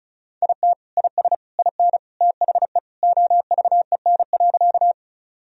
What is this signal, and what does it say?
Morse code, heard as a keyed tone, about 34 wpm, 700 Hz. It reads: IT IS IN THE OVEN.
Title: Sentences from Top 1000 Words - Review Part 4 - 35wpm